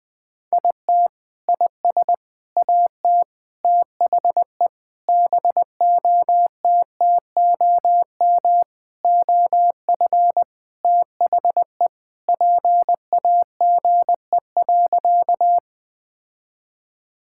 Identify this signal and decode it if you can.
Morse code, heard as a keyed tone, about 20 wpm, 700 Hz. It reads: IT IS AT THE BOTTOM OF THE PAGE.
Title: Sentences with 4-Word Phrases - Spoken Beginning - 20wpm